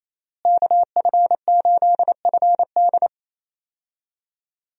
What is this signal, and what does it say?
Morse code, heard as a keyed tone, about 27 wpm, 700 Hz. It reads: KF8FD
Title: USA Call Signs - 5 Characters - Repeated 3x - 28wpm